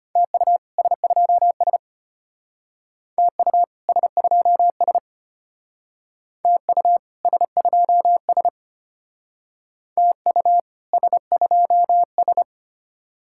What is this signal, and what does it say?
Morse code, heard as a keyed tone, about 32 wpm, 700 Hz. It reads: TU S2S TU S2S TU S2S TU S2S